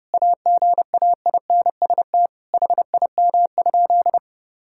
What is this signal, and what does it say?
Morse code, heard as a keyed tone, about 30 wpm, 700 Hz. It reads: AGAINST HIM?
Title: Sentences from Top 1000 Words - Review Part 6 - 30wpm